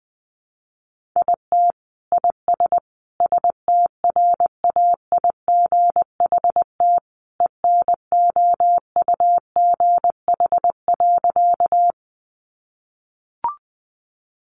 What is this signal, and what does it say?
Morse code, heard as a keyed tone, about 20 wpm, 700 Hz. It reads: IT IS STRAIGHT ENOUGH.
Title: Sentences from Top 1000 Words 20wpm